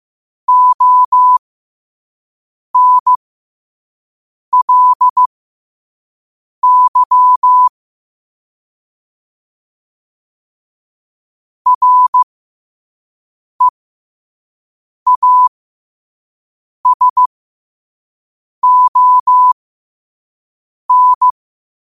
Morse code, heard as a keyed tone, about 15 wpm, spaced as for 5 wpm, 1000 Hz. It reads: ONLY REASON
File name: qotdfiles / 5wpm